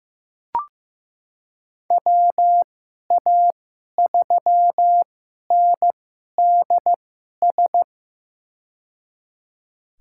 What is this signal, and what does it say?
Morse code, heard as a keyed tone, about 15 wpm, spaced as for 11 wpm, 700 Hz. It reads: WA3NDS
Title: USA Call Signs - 6 Characters - 2x Character Spacing - 15wpm